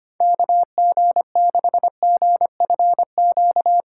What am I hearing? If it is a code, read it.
Morse code, heard as a keyed tone, about 25 wpm, 700 Hz. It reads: KG6GFQ
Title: USA Call Signs - 6 Characters - 4x Character Spacing - 25wpm